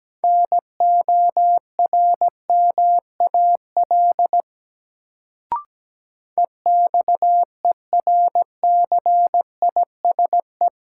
Morse code, heard as a keyed tone, about 17 wpm, 700 Hz. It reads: NORMAL EXERCISE
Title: Sentences from Top 1200 Words - New Word Drill Alt - 17wpm